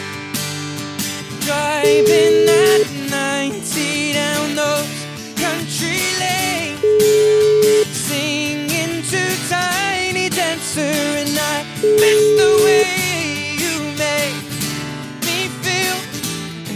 Ringback tone (Europe, 425 Hz), with noise.